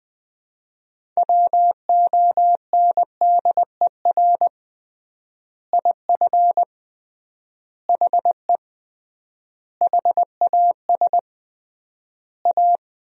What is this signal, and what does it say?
Morse code, heard as a keyed tone, about 20 wpm, 700 Hz. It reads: WONDER IF HE HAS A
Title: 3x Word-Spacing - Sentences from Top 500 Words - 20wpm